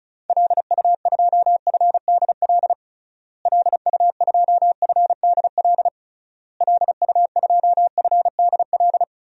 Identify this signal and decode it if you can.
Morse code, heard as a keyed tone, about 34 wpm, 700 Hz. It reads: LU2FDL LU2FDL LU2FDL
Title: DXCC-100 Call Signs - 6 Characters - Repeated 3x - 35wpm